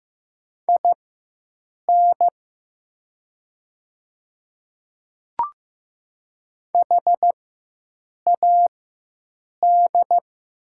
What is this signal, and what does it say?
Morse code, heard as a keyed tone, about 15 wpm, spaced as for 7 wpm, 700 Hz. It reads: IN HAD
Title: 4x Character Spacing - Top 300 Words - 15wpm